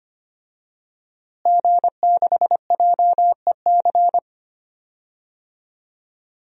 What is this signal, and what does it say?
Morse code, heard as a keyed tone, about 25 wpm, 700 Hz. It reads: G6JEC
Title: DXCC-100 Call Signs - 5 Characters - Repeated 2x - 25wpm